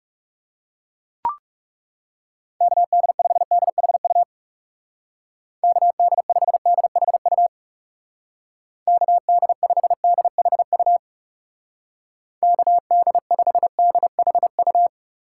Morse code, heard as a keyed tone, about 38 wpm, 700 Hz. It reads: KD5DHU KD5DHU KD5DHU KD5DHU